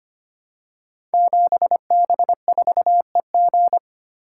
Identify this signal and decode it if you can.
Morse code, heard as a keyed tone, about 25 wpm, 690 Hz. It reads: 7B4EG
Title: Code Groups - 5 Letter:Numbers - 25wpm